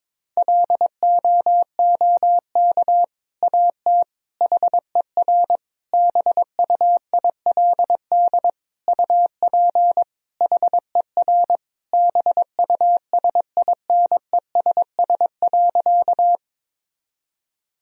Morse code, heard as a keyed tone, about 22 wpm, 700 Hz. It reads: LOOK AT HER BUILD UP HER BUSINESS.
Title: Sentences from Top 1000 Words - Review Part 1 - 22wpm